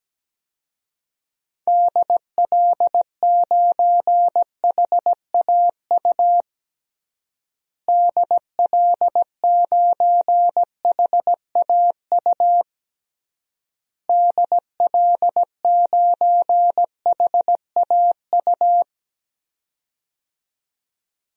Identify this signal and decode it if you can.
Morse code, heard as a keyed tone, about 17 wpm, 700 Hz. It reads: DL9HAU DL9HAU DL9HAU
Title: DXCC-100 Call Signs - 6 Characters - Repeated 3x - 17wpm